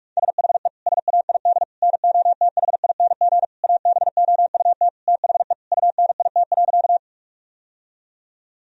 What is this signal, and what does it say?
Morse code, heard as a keyed tone, about 46 wpm, 700 Hz. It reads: SHE SAID NOTHING ABOUT THE UNIT.
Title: Sentences from Top 1000 Words - Review Part 2 - 45wpm